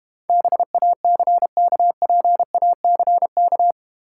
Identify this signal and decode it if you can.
Morse code, heard as a keyed tone, about 31 wpm, 700 Hz. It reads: BACKPACK
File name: practice / 25wpm / Difficult Words - Speed-Racing with Hints - 25wpm